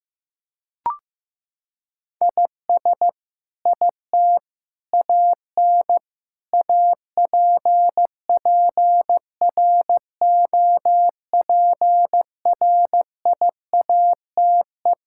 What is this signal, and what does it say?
Morse code, heard as a keyed tone, about 15 wpm, 700 Hz. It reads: IS IT AN APPROPRIATE
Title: Sentences from Top 1000 Words - Review Part 5 - 15wpm